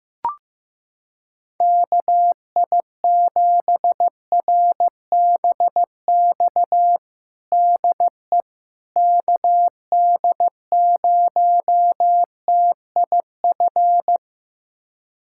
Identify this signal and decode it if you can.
Morse code, heard as a keyed tone, about 15 wpm, 700 Hz. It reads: KI7RBX DE KD0TIF
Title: USA Callsign DE Callsign - 6 Characters - 15wpm